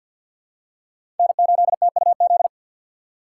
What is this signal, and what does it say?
Morse code, heard as a keyed tone, about 53 wpm, 700 Hz. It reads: N8TUZ
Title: USA Call Signs - 6x Character Spacing - 50wpm